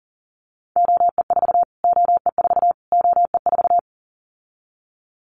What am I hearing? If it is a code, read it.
Morse code, heard as a keyed tone, about 40 wpm, 700 Hz. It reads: OE4 OE4 OE4
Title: Callsign Prefixes for Top 25 DXCC - Repeated 3x - 40wpm